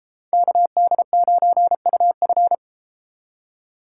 Morse code, heard as a keyed tone, about 31 wpm, 700 Hz. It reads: KD9UF